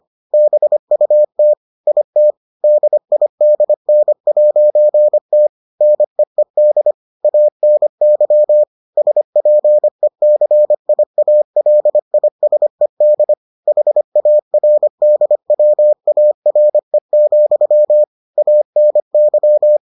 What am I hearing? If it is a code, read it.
Morse code, heard as a keyed tone, about 25 wpm, 600 Hz. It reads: BUT IT DIDN'T NEED ANY SPECIALISED HARDWARE, ANY